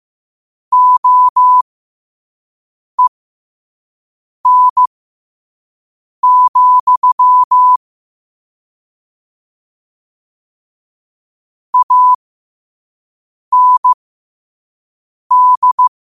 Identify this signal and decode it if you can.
Morse code, heard as a keyed tone, about 15 wpm, spaced as for 5 wpm, 1000 Hz. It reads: OEN, AND
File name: qotdfiles / 5wpm